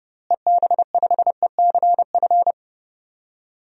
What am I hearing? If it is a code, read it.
Morse code, heard as a keyed tone, about 30 wpm, 700 Hz. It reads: EB5ECF
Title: DXCC-100 Call Signs - 6 Characters - Repeated 2x - 30wpm